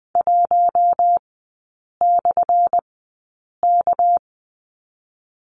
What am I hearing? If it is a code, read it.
Morse code, heard as a keyed tone, about 20 wpm, 700 Hz. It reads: 1 / K
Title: 198 Lesson 32e - Letter k - Three Characters 20wpm